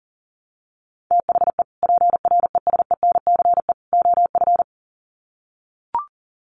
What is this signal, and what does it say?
Morse code, heard as a keyed tone, about 40 wpm, 700 Hz. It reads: THE PRESENCE OF
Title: Sets of 3 Words 40wpm